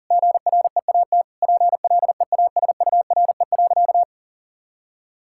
Morse code, heard as a keyed tone, about 37 wpm, 700 Hz. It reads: GREAT PLEASURE.
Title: Sentences from Top 1600 Words 40wpm